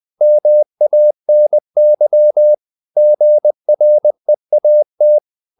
Morse code, heard as a keyed tone, about 20 wpm, 600 Hz. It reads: MANY GREAT
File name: the old man and the sea / the old man and the sea part 1 20wpm